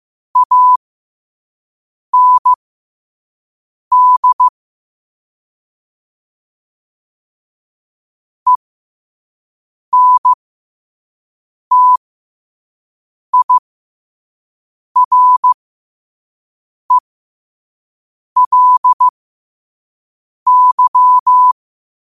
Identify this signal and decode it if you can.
Morse code, heard as a keyed tone, about 15 wpm, spaced as for 5 wpm, 1000 Hz. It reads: AND ENTIRELY
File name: qotdfiles / 5wpm